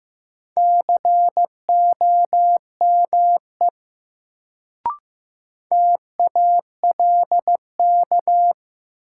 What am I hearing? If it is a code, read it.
Morse code, heard as a keyed tone, about 15 wpm, 700 Hz. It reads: COME TALK
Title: Sets of 2 Words - New Word Drill - 15wpm